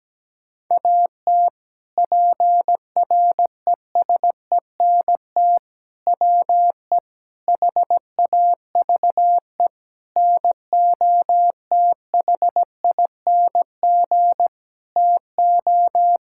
Morse code, heard as a keyed tone, about 17 wpm, 700 Hz. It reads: AT PRESENT WE HAVE NOTHING TO
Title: Sentences from Top 1000 Words - Review Part 5 - 17wpm